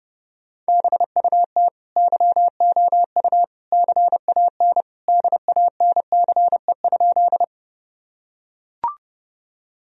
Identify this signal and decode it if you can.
Morse code, heard as a keyed tone, about 30 wpm, 700 Hz. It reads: BUT YOU CAN DANCE?
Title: Sentences from Top 1000 Words - Review Part 4 - 30wpm